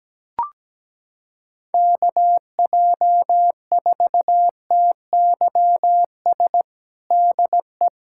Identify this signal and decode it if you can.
Morse code, heard as a keyed tone, about 17 wpm, 700 Hz. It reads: KJ4TYS DE
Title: USA Callsign DE Callsign - 6 Characters - 17wpm